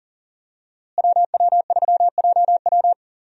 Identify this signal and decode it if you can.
Morse code, heard as a keyed tone, about 40 wpm, 700 Hz. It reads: WW3JW
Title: USA Call Signs - 5 Characters - Repeated 3x - 40wpm